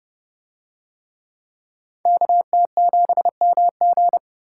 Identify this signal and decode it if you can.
Morse code, heard as a keyed tone, about 30 wpm, 700 Hz. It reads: KT7MG